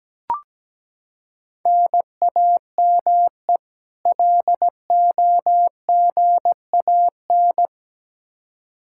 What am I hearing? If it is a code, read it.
Morse code, heard as a keyed tone, about 17 wpm, 700 Hz. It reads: NAME LOGAN
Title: Practice QSO - Exchange 1 - Beginning - 17wpm